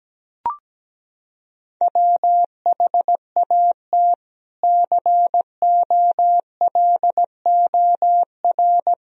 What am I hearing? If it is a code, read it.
Morse code, heard as a keyed tone, about 17 wpm, 700 Hz. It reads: WHAT COLOR
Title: Sentences from Top 1000 Words - Review Part 2 - 17wpm